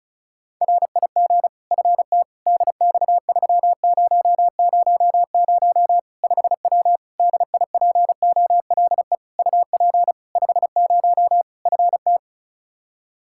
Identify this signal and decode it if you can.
Morse code, heard as a keyed tone, about 34 wpm, 700 Hz. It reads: RIG FT DX3000 5W DIPOLE UP 50 FT
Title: Practice QSO - Exchange 2 - Info - 35wpm